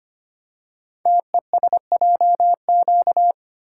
Morse code, heard as a keyed tone, about 25 wpm, 700 Hz. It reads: TESJQ
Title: Code Groups - 5 Letters - 25wpm